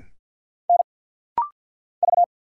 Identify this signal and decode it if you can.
Morse code, heard as a keyed tone, about 50 wpm, 690 Hz. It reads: N V
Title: Single Letters - Rapid-Fire 50wpm